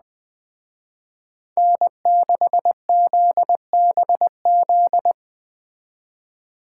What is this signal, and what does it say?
Morse code, heard as a keyed tone, about 21 wpm, 700 Hz. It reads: N6ZBZ